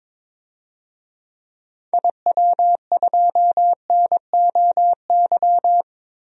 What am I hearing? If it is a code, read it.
Morse code, heard as a keyed tone, about 22 wpm, 700 Hz. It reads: IW2NOY
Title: DXCC-100 Call Signs - 6 Characters - Repeated 2x - 22wpm